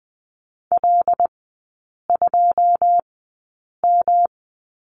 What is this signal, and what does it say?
Morse code, heard as a keyed tone, about 20 wpm, spaced as for 8 wpm, 700 Hz. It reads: L2M
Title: 107 Lesson 18e - Letter m - Three Characters 20wpm